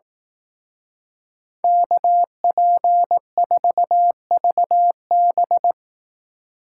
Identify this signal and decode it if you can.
Morse code, heard as a keyed tone, about 18 wpm, 700 Hz. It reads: KP4VB
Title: Speed Racing - USA Call Signs 15wpm